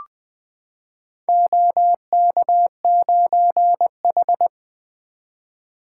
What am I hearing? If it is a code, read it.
Morse code, heard as a keyed tone, about 20 wpm, 700 Hz. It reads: OK9H
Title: Four Letter-Number Combinations - 20wpm